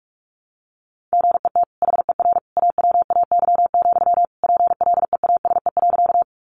Morse code, heard as a keyed tone, about 44 wpm, 700 Hz. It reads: GET HER AWAY, PLEASE.
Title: Sentences from Top 1500 Words 45wpm